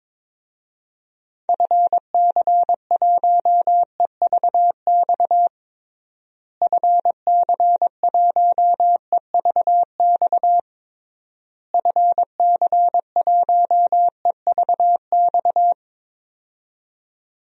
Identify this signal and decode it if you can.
Morse code, heard as a keyed tone, about 22 wpm, 700 Hz. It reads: FC1EVX FC1EVX FC1EVX
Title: DXCC-100 Call Signs - 6 Characters - Repeated 3x - 22wpm